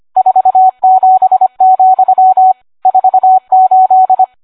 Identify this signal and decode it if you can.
Morse code, heard as a keyed tone, about 25 wpm, 750 Hz, with radio noise. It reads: 47, 48